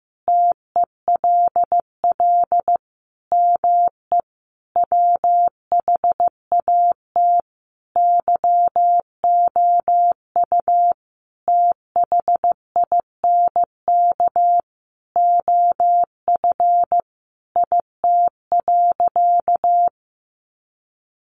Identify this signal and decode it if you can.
Morse code, heard as a keyed tone, about 15 wpm, 700 Hz. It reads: TELL ME WHAT YOU THINK OF IT.
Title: Sentences from Top 200 Words 15wpm